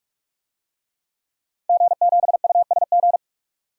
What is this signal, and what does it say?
Morse code, heard as a keyed tone, about 46 wpm, 700 Hz. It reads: G7USG
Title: DXCC-100 Call Signs - 5 Characters - 45wpm